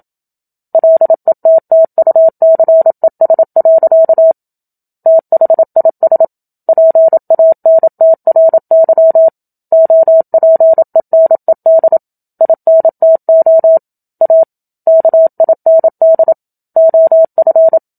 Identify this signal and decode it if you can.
Morse code, heard as a keyed tone, about 27 wpm, 650 Hz. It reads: LETTUCES. THIS PANTRY OPENED INTO A KIND OF
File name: Book Two / 01-Chapter 1 - 27 WPM